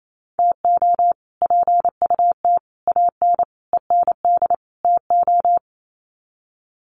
Morse code, heard as a keyed tone, about 28 wpm, 700 Hz. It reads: TO PUT AN END TO
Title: Sets of 5 Words Encore2 28wpm